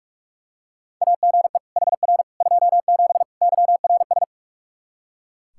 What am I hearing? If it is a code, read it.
Morse code, heard as a keyed tone, about 42 wpm, 700 Hz. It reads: AGE HR 27 YRS
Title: Practice QSO - Exchange 3 - More Info - 45wpm